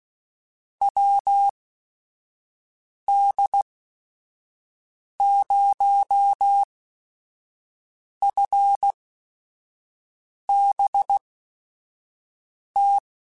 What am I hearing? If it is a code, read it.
Morse code, heard as a keyed tone, about 16 wpm, spaced as for 5 wpm, 800 Hz. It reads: WD0FBT
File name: QSOs / QSOs-mp3s / WD0FBT 1@5